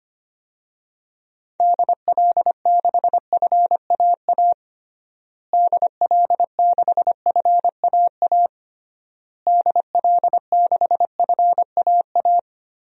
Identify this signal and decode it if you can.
Morse code, heard as a keyed tone, about 25 wpm, 700 Hz. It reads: DL6FAA DL6FAA DL6FAA
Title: DXCC-100 Call Signs - 6 Characters - Repeated 3x - 25wpm